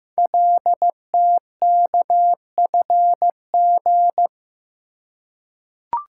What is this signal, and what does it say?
Morse code, heard as a keyed tone, about 15 wpm, 700 Hz. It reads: LTKFG
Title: Code Groups - 5 Letters - 15wpm